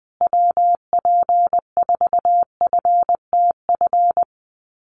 Morse code, heard as a keyed tone, about 20 wpm, 700 Hz. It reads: WP4FTF
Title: USA Call Signs Repeated 3x 20wpm